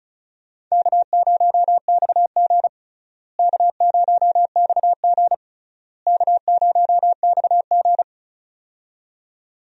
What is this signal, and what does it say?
Morse code, heard as a keyed tone, about 36 wpm, 700 Hz. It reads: K0XG K0XG K0XG
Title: USA Call Signs - 4 Characters - Repeated 3x - 35wpm